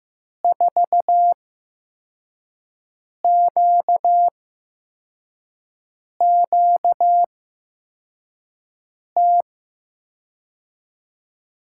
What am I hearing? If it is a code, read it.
Morse code, heard as a keyed tone, about 15 wpm, spaced as for 4 wpm, 700 Hz. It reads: 4QQT
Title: USA Call Signs - 5 Characters - 8x Character Spacing - 15wpm